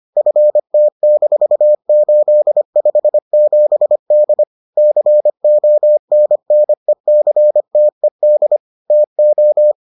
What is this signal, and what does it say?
Morse code, heard as a keyed tone, about 25 wpm, 600 Hz. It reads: FT-857D CONNECTED TO